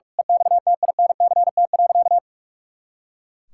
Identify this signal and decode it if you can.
Morse code, heard as a keyed tone, about 45 wpm, 700 Hz. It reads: EXTINCT.
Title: Pangram Challenge v2 - 45wpm